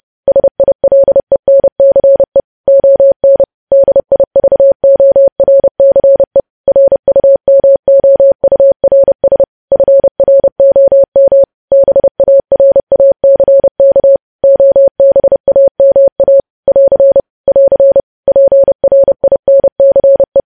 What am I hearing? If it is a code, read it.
Morse code, heard as a keyed tone, about 30 wpm, 550 Hz. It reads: SILENCE ON DIVORCE RUMOURS FROM BARACK OBAMA <AR> <AR> PRINCE
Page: news headlines in morse code at 30 wpm